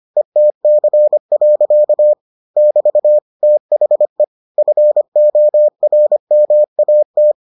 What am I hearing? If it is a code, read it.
Morse code, heard as a keyed tone, about 25 wpm, 600 Hz. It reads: ETC. <BT> THE FORMAT